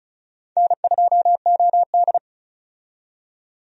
Morse code, heard as a keyed tone, about 36 wpm, 700 Hz. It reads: N2OD
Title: USA Call Signs - 4 Characters - 35wpm